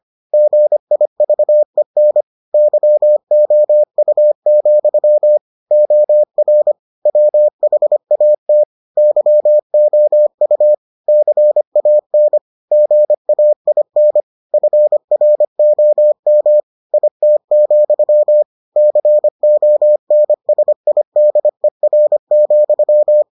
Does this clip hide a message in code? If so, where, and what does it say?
Yes: Morse code from 0.3 to 23.2 s, heard as a keyed tone, about 25 wpm, 600 Hz. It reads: GIVEN YOU, OR WHAT YOU CAN GAIN FROM IT, CONSIDER,